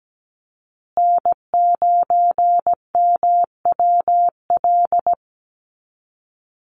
Morse code, heard as a keyed tone, about 17 wpm, 700 Hz. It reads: N9MWL
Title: USA Call Signs Repeated 2x 17wpm